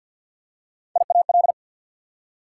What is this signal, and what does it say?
Morse code, heard as a keyed tone, about 52 wpm, 690 Hz. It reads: IAL
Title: Prefixes and Suffixes 50wpm